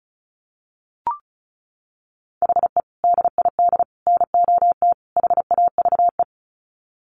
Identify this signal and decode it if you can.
Morse code, heard as a keyed tone, about 34 wpm, 700 Hz. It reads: HE DID NOT HAVE
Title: Sets of 4 Words 35wpm